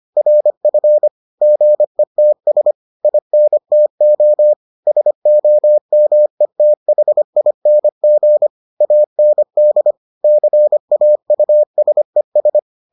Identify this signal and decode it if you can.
Morse code, heard as a keyed tone, about 25 wpm, 600 Hz. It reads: RF GETS INTO SOMETHING AND CAUSES